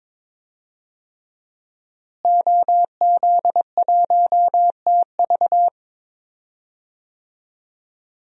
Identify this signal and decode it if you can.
Morse code, heard as a keyed tone, about 22 wpm, 700 Hz. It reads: OZ1TV
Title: DXCC-100 Call Signs - 5 Characters - Repeated 3x - 22wpm